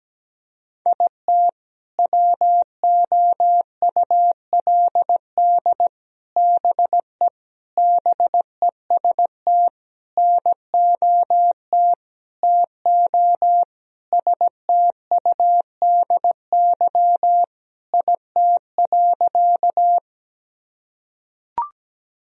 Morse code, heard as a keyed tone, about 17 wpm, 700 Hz. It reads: IT WOULD BE BEST NOT TO STUDY IT.
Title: Sentences from Top 1000 Words - Review Part 1 - 17wpm